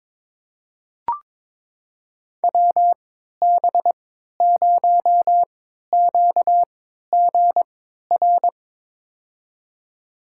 Morse code, heard as a keyed tone, about 22 wpm, spaced as for 13 wpm, 700 Hz. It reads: WB0QGR
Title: USA Call Signs - 6 Characters - 3x Character Spacing - 22wpm